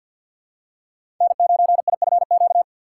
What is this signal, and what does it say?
Morse code, heard as a keyed tone, about 52 wpm, 700 Hz. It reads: N9IFQ
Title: USA Call Signs - 3x Character Spacing - 50wpm